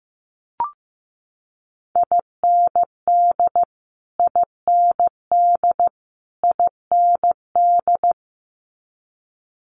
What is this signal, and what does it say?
Morse code, heard as a keyed tone, about 15 wpm, 700 Hz. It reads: IND IND IND
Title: Multi-letter Phonemes Repeated 3x 15wpm